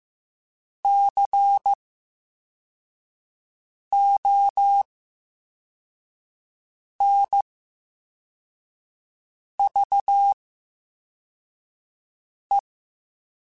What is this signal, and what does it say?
Morse code, heard as a keyed tone, about 15 wpm, spaced as for 4 wpm, 800 Hz. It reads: CONVE
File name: Special INGs / Special INGs-mp3s / Slow ing @13